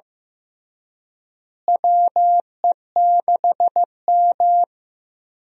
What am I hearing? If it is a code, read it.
Morse code, heard as a keyed tone, about 15 wpm, 700 Hz. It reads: WE6M
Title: USA Call Signs - 4 Characters - Repeated 3x - 15wpm